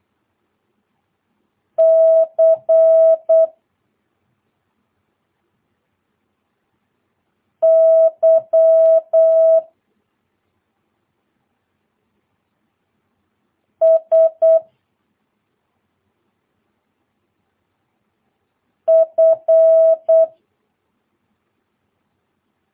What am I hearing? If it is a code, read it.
Morse code, heard as a keyed tone, about 8 wpm, 650 Hz. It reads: C Y S F
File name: jh8cbh / 2022cw